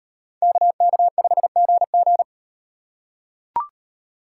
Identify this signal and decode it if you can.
Morse code, heard as a keyed tone, about 38 wpm, 700 Hz. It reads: KK5GG